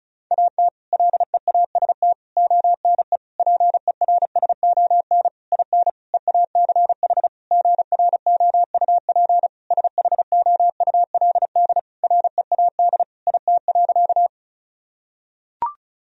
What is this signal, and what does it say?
Morse code, heard as a keyed tone, about 36 wpm, 700 Hz. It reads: AT LEAST ONE PERSON IN EACH GROUP SHOULD READ IT.
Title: Sentences from Top 1000 Words - Review Part 1 - 35wpm